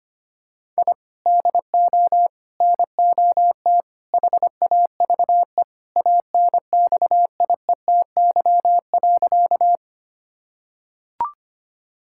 Morse code, heard as a keyed tone, about 25 wpm, 700 Hz. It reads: I DO NOT HAVE ANXIETY.
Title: Sentences from Top 1600 Words 25wpm